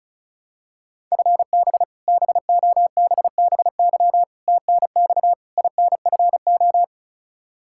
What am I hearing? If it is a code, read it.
Morse code, heard as a keyed tone, about 36 wpm, 700 Hz. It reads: FB BOBBY TNX INFO
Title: Practice QSO - Exchange 3 - More Info - 35wpm